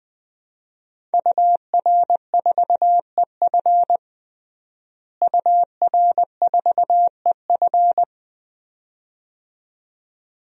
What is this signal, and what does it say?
Morse code, heard as a keyed tone, about 20 wpm, 700 Hz. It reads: UR4EF UR4EF
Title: DXCC-100 Call Signs - 5 Characters - Repeated 2x - 20wpm